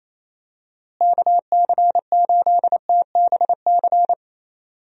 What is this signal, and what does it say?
Morse code, heard as a keyed tone, about 28 wpm, 700 Hz. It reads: KC8TBC
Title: USA Call Signs - 6 Characters - Repeated 2x - 28wpm